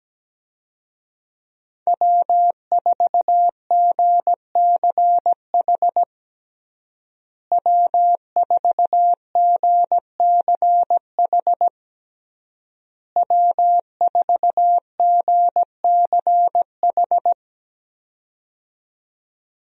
Morse code, heard as a keyed tone, about 17 wpm, 700 Hz. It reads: W4GCH W4GCH W4GCH
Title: USA Call Signs - 5 Characters - Repeated 3x - 17wpm